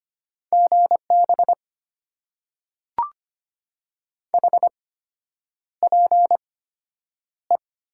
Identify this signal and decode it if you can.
Morse code, heard as a keyed tone, about 25 wpm, spaced as for 9 wpm, 700 Hz. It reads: GB HPE
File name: Top 90 QSO Elements - 8x Character Spacing - 25wpm